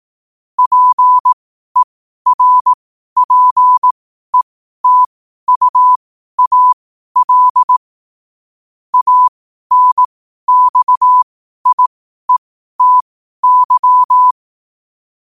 Morse code, heard as a keyed tone, about 18 wpm, spaced as for 12 wpm, 1000 Hz. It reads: PERPETUAL ANXIETY
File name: qotdfiles / 12wpm